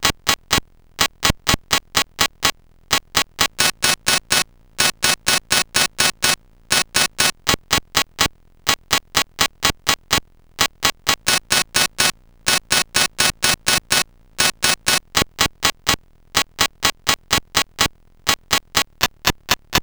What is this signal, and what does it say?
There is no code.